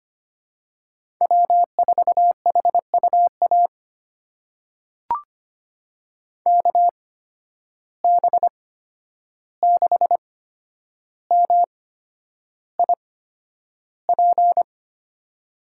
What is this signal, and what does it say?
Morse code, heard as a keyed tone, about 25 wpm, spaced as for 10 wpm, 700 Hz. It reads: W4HUA KB6MIP
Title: USA Call Signs - 8x Character Spacing - 25wpm